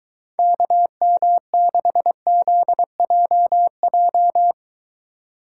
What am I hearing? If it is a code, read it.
Morse code, heard as a keyed tone, about 22 wpm, 700 Hz. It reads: KM6ZJJ